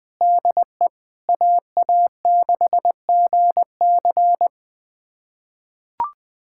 Morse code, heard as a keyed tone, about 20 wpm, 700 Hz. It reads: DE AA6GC
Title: USA Callsign DE Callsign - 5 Characters - 20wpm